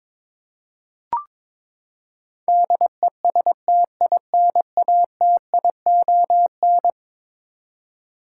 Morse code, heard as a keyed tone, about 22 wpm, 700 Hz. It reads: DESTINATION
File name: practice / 22wpm / Eleven Letter Words - 22wpm